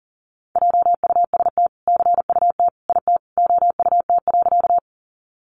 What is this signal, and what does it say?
Morse code, heard as a keyed tone, about 40 wpm, 700 Hz. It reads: JUST CUT IT OUT.
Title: Sentences from Top 300 Words 40wpm